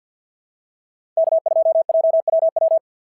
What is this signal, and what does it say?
Morse code, heard as a keyed tone, about 52 wpm, 690 Hz. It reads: K2JWW
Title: SOTA QSO Practice 50wpm